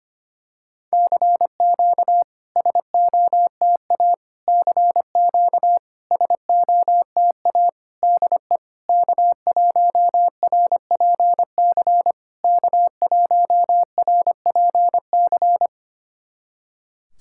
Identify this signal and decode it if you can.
Morse code, heard as a keyed tone, about 25 wpm, 700 Hz. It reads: CQ SOTA CQ SOTA DE K1RPC K1RPC
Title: SOTA QSO Practice 25wpm